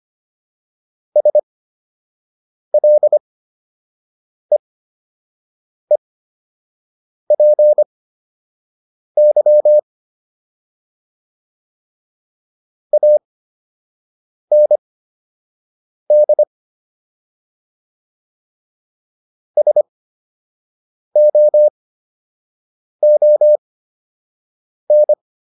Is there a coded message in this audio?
Yes: Morse code from 1.2 to 25.1 s, heard as a keyed tone, about 25 wpm, spaced as for 6 wpm, 600 Hz. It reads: SLEEPY AND SOON